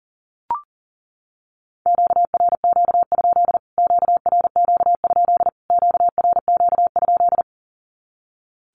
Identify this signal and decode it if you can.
Morse code, heard as a keyed tone, about 40 wpm, 700 Hz. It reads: QRQ? QRQ? QRQ?
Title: Q-Codes Repeated 3x 40wpm